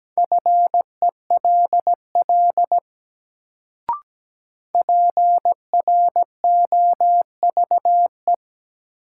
Morse code, heard as a keyed tone, about 17 wpm, 700 Hz. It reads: FELL PROVE
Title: Sentences from Top 1600 Words - New Word Drill - 17wpm